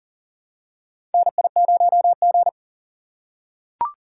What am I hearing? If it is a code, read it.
Morse code, heard as a keyed tone, about 40 wpm, 700 Hz. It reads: NI0G
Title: USA Call Signs - 4 Characters - 4x Character Spacing - 40wpm